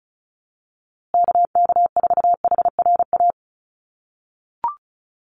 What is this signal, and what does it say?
Morse code, heard as a keyed tone, about 34 wpm, 700 Hz. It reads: KK4HRA
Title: USA Call Signs Repeated 2x 35wpm